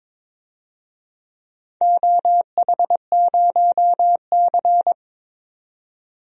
Morse code, heard as a keyed tone, about 22 wpm, 700 Hz. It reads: OH0C
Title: DXCC-100 Call Signs - 4 Characters - 22wpm